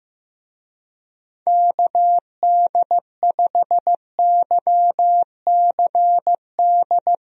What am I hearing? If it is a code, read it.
Morse code, heard as a keyed tone, about 15 wpm, 700 Hz. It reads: KD5YCD
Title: USA Call Signs - 6 Characters - Repeated 2x - 15wpm